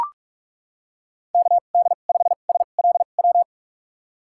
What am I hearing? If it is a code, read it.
Morse code, heard as a keyed tone, about 41 wpm, spaced as for 31 wpm, 700 Hz. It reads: KD5SLW
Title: USA Call Signs - 2x Character Spacing - 45wpm